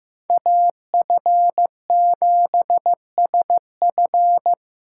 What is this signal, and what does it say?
Morse code, heard as a keyed tone, about 18 wpm, 700 Hz. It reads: AF7SF
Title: Speed Racing - USA 2x2 Call Signs 15wpm